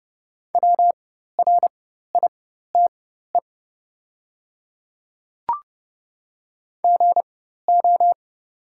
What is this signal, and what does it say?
Morse code, heard as a keyed tone, about 30 wpm, spaced as for 14 wpm, 690 Hz. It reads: WRITE GO
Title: 4x Character Spacing - Top 100 Words - 30wpm